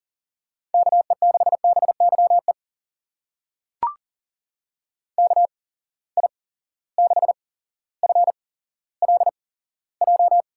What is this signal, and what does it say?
Morse code, heard as a keyed tone, about 40 wpm, spaced as for 17 wpm, 700 Hz. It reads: KE6BYE KI6FLJ
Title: USA Call Signs - 6 Characters - 8x Character Spacing - 40wpm